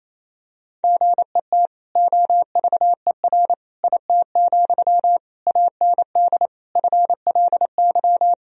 Morse code, heard as a keyed tone, about 28 wpm, 700 Hz. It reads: GET OVER IT, AND FLY
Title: Sentences from Top 1000 Words - Review Part 2 - 28wpm